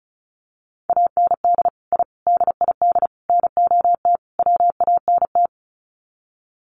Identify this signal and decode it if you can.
Morse code, heard as a keyed tone, about 34 wpm, 700 Hz. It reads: AND I DID NOT WANT
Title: Sets of 5 Words 35wpm